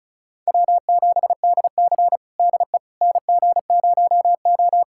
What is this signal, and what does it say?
Morse code, heard as a keyed tone, about 36 wpm, 700 Hz. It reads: W7DC DE NG0O
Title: USA Callsign DE Callsign - 4 Characters - 35wpm